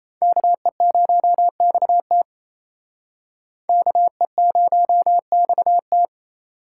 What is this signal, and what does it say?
Morse code, heard as a keyed tone, about 34 wpm, 700 Hz. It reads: KE0XT KE0XT
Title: Speed Racing - USA 2x2 Call Signs 28wpm